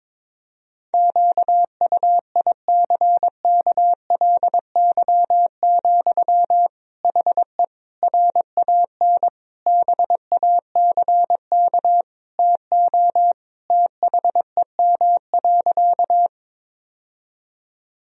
Morse code, heard as a keyed tone, about 22 wpm, 700 Hz. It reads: QUICKLY, HE RAN BACK TO THEM.
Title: Sentences from Top 1000 Words - Review Part 3 - 22wpm